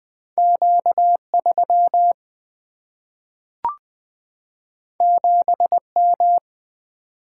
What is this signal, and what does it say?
Morse code, heard as a keyed tone, about 20 wpm, 700 Hz. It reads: Q3 7M
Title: Single-Letter Single-Number 20wpm